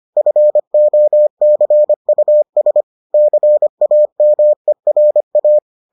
Morse code, heard as a keyed tone, about 25 wpm, 600 Hz. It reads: FOCUS CAMERA